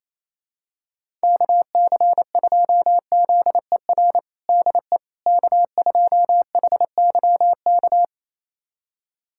Morse code, heard as a keyed tone, about 28 wpm, 700 Hz. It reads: KC2ZER DE K2HYK
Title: USA Callsign DE Callsign 28wpm